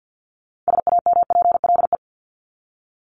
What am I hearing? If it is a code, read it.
Morse code, heard as a keyed tone, about 48 wpm, 700 Hz. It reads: SAMPLE
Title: Top 1501-1750 Words 50wpm